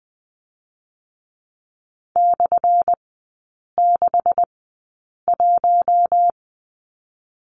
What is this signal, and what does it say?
Morse code, heard as a keyed tone, about 20 wpm, 700 Hz. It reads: / 6 1
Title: 178 Lesson 29e - Slash - Three Characters 20wpm